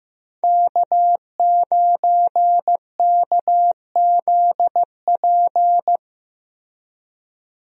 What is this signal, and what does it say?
Morse code, heard as a keyed tone, about 15 wpm, 700 Hz. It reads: K9KZP
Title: USA Call Signs - 5 Characters - Repeated 3x - 15wpm